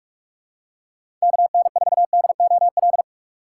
Morse code, heard as a keyed tone, about 44 wpm, 700 Hz. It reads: KN4DOL
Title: USA Call Signs - 6x Character Spacing - 45wpm